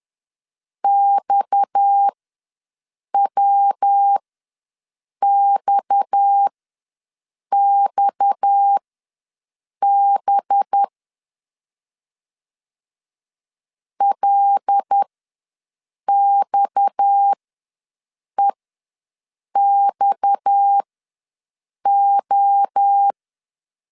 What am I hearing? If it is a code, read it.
Morse code, heard as a keyed tone, about 10 wpm, spaced as for 6 wpm, 800 Hz. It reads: XWXXB LXEXO
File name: LEZIONI / 6 PKX0 / 6 PKX ws 10wpm 3706